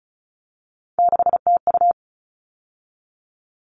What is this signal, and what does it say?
Morse code, heard as a keyed tone, about 34 wpm, 700 Hz. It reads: 6TU
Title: All Cut Numbers - 3-Digit 35wpm